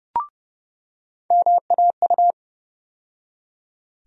Morse code, heard as a keyed tone, about 30 wpm, 700 Hz. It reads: MAU
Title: USA Call Sign Postfixes - 3 Letters 30wpm